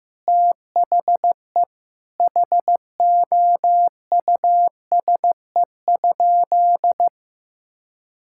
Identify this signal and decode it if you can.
Morse code, heard as a keyed tone, about 16 wpm, 700 Hz. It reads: THE HOUSE?
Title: Sentences from Top 1600 Words 15wpm